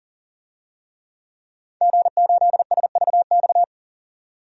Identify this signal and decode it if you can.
Morse code, heard as a keyed tone, about 40 wpm, 700 Hz. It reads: G8SVX
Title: DXCC-100 Call Signs - 5 Characters - 40wpm